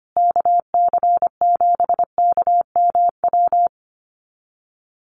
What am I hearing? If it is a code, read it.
Morse code, heard as a keyed tone, about 25 wpm, 700 Hz. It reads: KC7KMW
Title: USA Call Signs Repeated 3x 25wpm